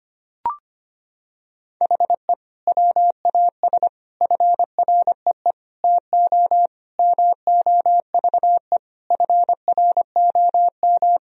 Morse code, heard as a keyed tone, about 25 wpm, 700 Hz. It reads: HE WAS FREE TO MOVE FROM
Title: Sentences from Top 1000 Words - Review Part 1 - 25wpm